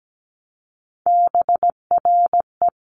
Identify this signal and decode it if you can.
Morse code, heard as a keyed tone, about 17 wpm, 700 Hz. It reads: BRE
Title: Sentences from Top 1500 Words 17wpm